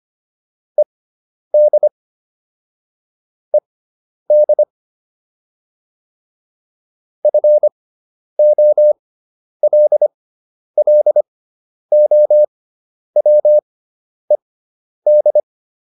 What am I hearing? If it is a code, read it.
Morse code, heard as a keyed tone, about 25 wpm, spaced as for 10 wpm, 600 Hz. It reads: ED ED FOLLOWED